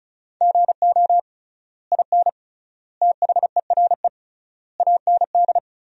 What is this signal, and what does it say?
Morse code, heard as a keyed tone, about 34 wpm, 700 Hz. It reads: GO IN THERE AND
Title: 3x Word-Spacing - Sentences from Top 500 Words - 35wpm